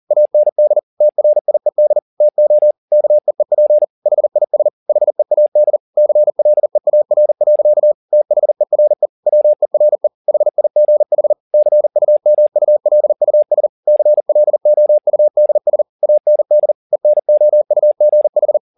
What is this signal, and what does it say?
Morse code, heard as a keyed tone, about 40 wpm, 600 Hz. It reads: AND TRIED TO KEEP HIS HEAD CLEAR. THERE WERE HIGH CUMULUS CLOUDS AND ENOUGH
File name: the old man and the sea 2 40wpm